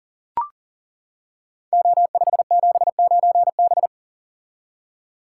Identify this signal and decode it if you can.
Morse code, heard as a keyed tone, about 39 wpm, 700 Hz. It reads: O579B
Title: Code Groups - 5 Letter:Numbers - 40wpm